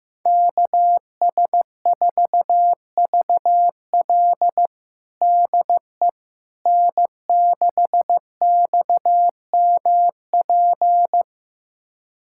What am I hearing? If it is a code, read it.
Morse code, heard as a keyed tone, about 15 wpm, 700 Hz. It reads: KS4VL DE N6XMP
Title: USA Callsign DE Callsign - 5 Characters - 15wpm